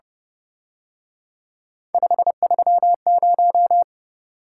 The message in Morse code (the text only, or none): 530